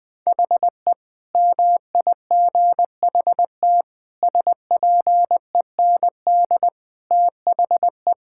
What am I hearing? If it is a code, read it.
Morse code, heard as a keyed tone, about 21 wpm, 700 Hz. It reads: HE MIGHT SPEND THE
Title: Sentences with 8-Word Phrases - 20wpm